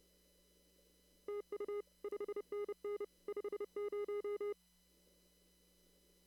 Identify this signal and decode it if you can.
Morse code, heard as a keyed tone, about 29 wpm, 410 Hz. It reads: TU 5NN 50
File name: CW / kt5j wpx cw-op-k5pi